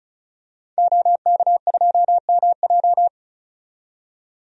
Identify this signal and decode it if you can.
Morse code, heard as a keyed tone, about 36 wpm, 700 Hz. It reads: OK2MJ
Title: DXCC-100 Call Signs - 5 Characters - 35wpm